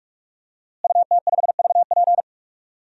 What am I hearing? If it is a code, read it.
Morse code, heard as a keyed tone, about 46 wpm, 700 Hz. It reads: UT5VP
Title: DXCC-100 Call Signs - 5 Characters - Repeated 3x - 45wpm